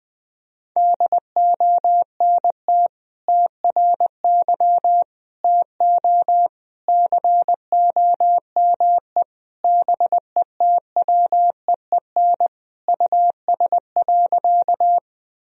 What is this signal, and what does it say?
Morse code, heard as a keyed tone, about 21 wpm, 700 Hz. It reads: DONT TRY TO COME BETWEEN US.
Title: Sentences from Top 1000 Words - Review Part 1 - 20wpm